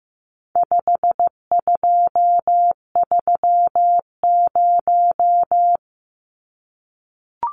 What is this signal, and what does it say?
Morse code, heard as a keyed tone, about 15 wpm, 700 Hz. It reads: 5230
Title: Numbers - 4-Digit 15wpm